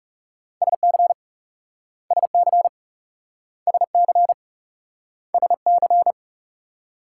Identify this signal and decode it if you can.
Morse code, heard as a keyed tone, about 38 wpm, 700 Hz. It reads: SC SC SC SC